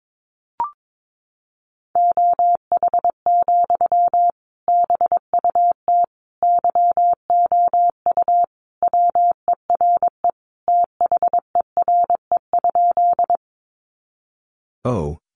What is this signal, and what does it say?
Morse code, heard as a keyed tone, about 22 wpm, 700 Hz. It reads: OH, BUT YOU WERE THERE?
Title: Sentences from Top 500 Words 22wpm